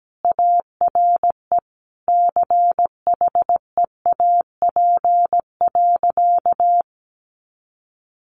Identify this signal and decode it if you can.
Morse code, heard as a keyed tone, about 17 wpm, 700 Hz. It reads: ARE CHEAP.
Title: Additional Easy Phrases 17wpm